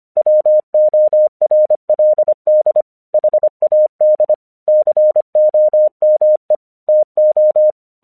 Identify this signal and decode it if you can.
Morse code, heard as a keyed tone, about 25 wpm, 610 Hz. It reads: WORLD HAD COME TO
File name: 1 English 25wpm / AQC04